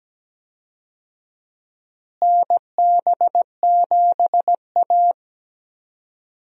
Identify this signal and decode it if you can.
Morse code, heard as a keyed tone, about 17 wpm, 700 Hz. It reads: NB7A
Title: USA Call Signs - 4 Characters - Repeated 2x - 17wpm